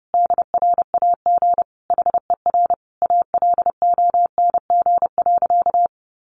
Morse code, heard as a keyed tone, about 30 wpm, 700 Hz. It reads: DRAG HER ALONG.
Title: Sentences from Top 1500 Words 30wpm